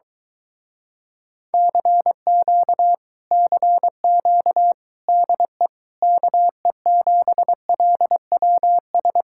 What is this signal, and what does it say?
Morse code, heard as a keyed tone, about 23 wpm, 700 Hz. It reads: CQ CQ DE KE7LWS